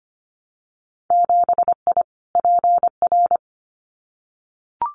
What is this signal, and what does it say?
Morse code, heard as a keyed tone, about 25 wpm, 700 Hz. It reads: 7I PR
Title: Winter Field Day Exchanges Encore 25wpm